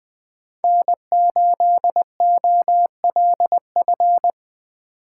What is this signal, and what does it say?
Morse code, heard as a keyed tone, about 20 wpm, 700 Hz. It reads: N8OLF
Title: USA Call Signs - 5 Characters - 3x Character Spacing - 20wpm